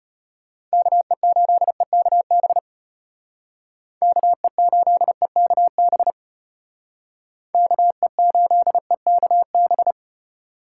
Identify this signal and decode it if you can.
Morse code, heard as a keyed tone, about 32 wpm, 700 Hz. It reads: KE8EKB KE8EKB KE8EKB